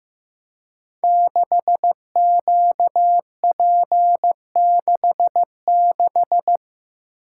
Morse code, heard as a keyed tone, about 15 wpm, 700 Hz. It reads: 6QP66
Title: Code Groups - 5 Letter:Numbers - 15wpm